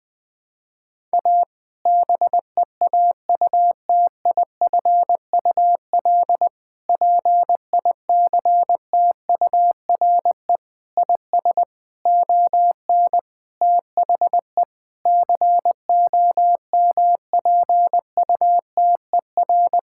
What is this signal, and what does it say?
Morse code, heard as a keyed tone, about 20 wpm, 700 Hz. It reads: A BEAUTIFUL PICTURE IS ON THE COMPUTER
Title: Sentences from Top 1000 Words - Review Part 6 - 20wpm